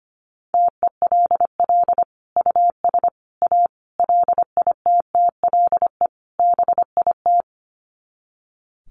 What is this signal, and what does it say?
Morse code, heard as a keyed tone, about 25 wpm, 700 Hz. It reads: TELL US A LITTLE BIT
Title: Sets of 5 Words Encore2 25wpm